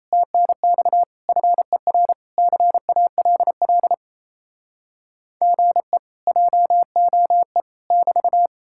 Morse code, heard as a keyed tone, about 37 wpm, 700 Hz. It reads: TNX FER CALL GE JOE <BT>